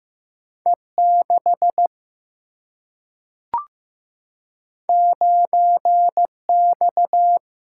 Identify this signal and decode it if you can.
Morse code, heard as a keyed tone, about 15 wpm, 700 Hz. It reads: E6 9X
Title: Single-Letter Single-Number 15wpm